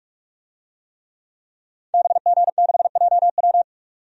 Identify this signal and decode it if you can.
Morse code, heard as a keyed tone, about 45 wpm, 700 Hz. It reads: BG6JW
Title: DXCC-100 Call Signs - 5 Characters - 45wpm